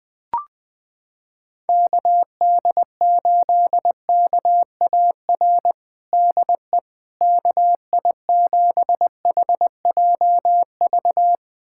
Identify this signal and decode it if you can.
Morse code, heard as a keyed tone, about 20 wpm, 700 Hz. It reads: KD8KAR DE KI7HJV
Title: USA Callsign DE Callsign - 6 Characters - 20wpm